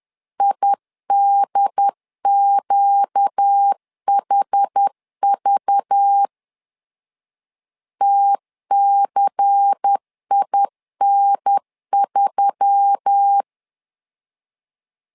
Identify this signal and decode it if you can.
Morse code, heard as a keyed tone, about 10 wpm, 800 Hz. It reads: IDQHV TCIN3
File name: LEZIONI / 5 LBVC / 5 LBVC 10wpm 2349